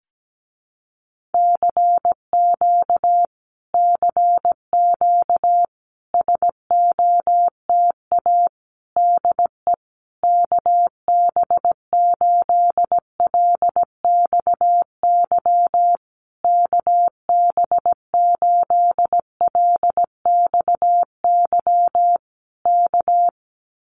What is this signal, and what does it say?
Morse code, heard as a keyed tone, about 17 wpm, 700 Hz. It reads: CQ CQ SOTA DE KB8LXY KB8LXY K
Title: Common QSO Phrases Encore 17wpm